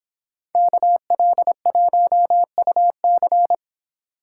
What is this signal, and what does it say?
Morse code, heard as a keyed tone, about 25 wpm, 700 Hz. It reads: KL1UC